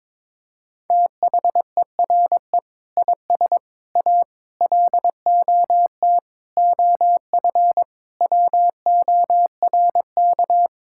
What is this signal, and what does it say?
Morse code, heard as a keyed tone, about 22 wpm, 700 Hz. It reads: THERE IS A LOT OF WORK
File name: Sentences with 7-Word Phrases - 22wpm